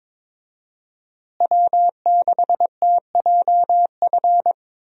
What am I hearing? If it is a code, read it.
Morse code, heard as a keyed tone, about 22 wpm, 700 Hz. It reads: W6TJF
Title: USA Call Signs - 5 Characters - Repeated 3x - 22wpm